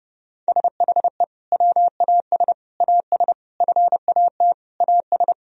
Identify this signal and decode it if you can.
Morse code, heard as a keyed tone, about 29 wpm, 700 Hz. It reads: SHE WAS AS FAT AS
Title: Sentences from Top 1000 Words - Review Part 4 - 30wpm